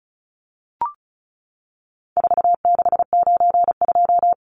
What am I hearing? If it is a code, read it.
Morse code, heard as a keyed tone, about 34 wpm, 700 Hz. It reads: 4692
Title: Numbers - 4-Digit 35wpm